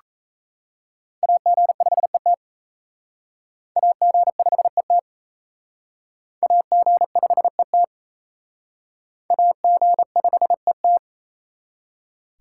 Morse code, heard as a keyed tone, about 34 wpm, 700 Hz. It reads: AG5ET AG5ET AG5ET AG5ET
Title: Speed Racing - USA 2x2 Call Signs 28wpm